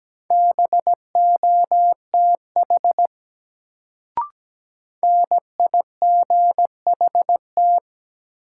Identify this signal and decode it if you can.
Morse code, heard as a keyed tone, about 17 wpm, 700 Hz. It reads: BOTH NIGHT
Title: Sets of 4 Words - New Word Drill - 17wpm